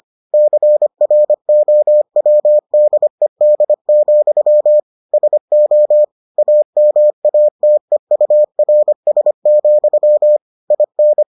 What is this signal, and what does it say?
Morse code, heard as a keyed tone, about 25 wpm, 600 Hz. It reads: CROWDED, SO AMATEURS, IN